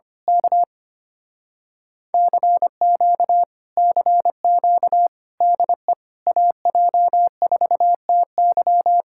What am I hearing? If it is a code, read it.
Morse code, heard as a keyed tone, about 24 wpm, 700 Hz. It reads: K CQ CQ DE AJ4TY
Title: Speed-Racing Common QSO Phrases Encore - 25wpm